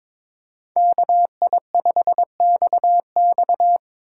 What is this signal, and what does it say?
Morse code, heard as a keyed tone, about 30 wpm, spaced as for 26 wpm, 700 Hz. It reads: KI5XX
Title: Speed Racing - USA 2x2 Call Signs 22wpm